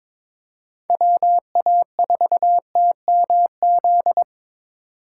Morse code, heard as a keyed tone, about 22 wpm, 700 Hz. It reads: WA4TMZ
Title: USA Call Signs - 3x Character Spacing - 22wpm